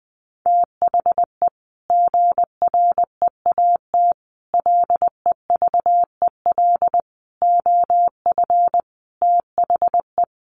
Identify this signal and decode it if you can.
Morse code, heard as a keyed tone, about 20 wpm, 700 Hz. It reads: THE GREAT LEVEL OF THE
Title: A Sound Of Thunder 20wpm